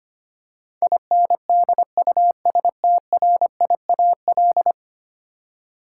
Morse code, heard as a keyed tone, about 25 wpm, 700 Hz. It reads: INDUSTRIAL
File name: Ten Letter Words - 25wpm